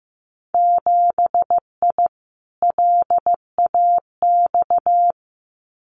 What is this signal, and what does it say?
Morse code, heard as a keyed tone, about 15 wpm, 700 Hz. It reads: 7I LAX
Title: Winter Field Day Exchanges Encore 15wpm